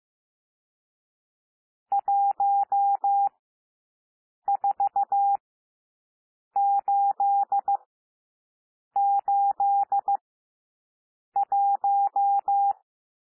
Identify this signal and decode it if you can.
Morse code, heard as a keyed tone, about 15 wpm, spaced as for 6 wpm, 800 Hz. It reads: 14881
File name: cwmp3 / N1 SOUND NUMBERS 15 wpm